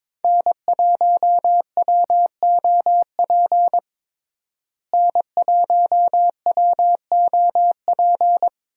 Morse code, heard as a keyed tone, about 21 wpm, 700 Hz. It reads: N1WOP N1WOP
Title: USA Call Signs - 5 Characters - Repeated 3x - 22wpm